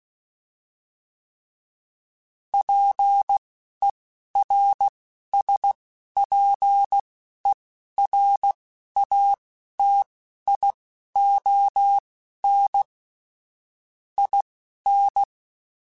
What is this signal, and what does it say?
Morse code, heard as a keyed tone, about 16 wpm, spaced as for 12 wpm, 800 Hz. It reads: PERSPERATION IN
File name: Quotes / Quotes-mp3s / Persperation @10